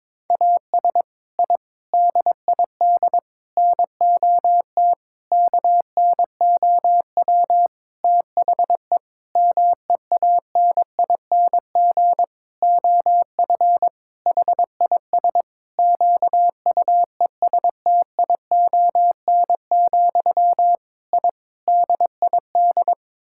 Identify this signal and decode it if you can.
Morse code, heard as a keyed tone, about 22 wpm, 700 Hz. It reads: AS I DID NOT KNOW THE MEANING OF HIS QUESTION, I DID
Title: Sentences from Top 1000 Words - Review Part 6 - 22wpm